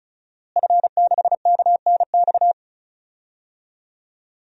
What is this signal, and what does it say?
Morse code, heard as a keyed tone, about 34 wpm, 690 Hz. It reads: F6KNX
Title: DXCC-100 Call Signs - 5 Characters - Repeated 3x - 35wpm